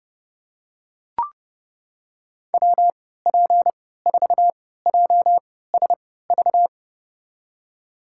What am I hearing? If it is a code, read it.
Morse code, heard as a keyed tone, about 30 wpm, spaced as for 17 wpm, 700 Hz. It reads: WP4JSV
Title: USA Call Signs - 6 Characters - 3x Character Spacing - 30wpm